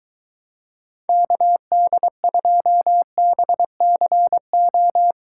Morse code, heard as a keyed tone, about 23 wpm, 700 Hz. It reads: KD2BCO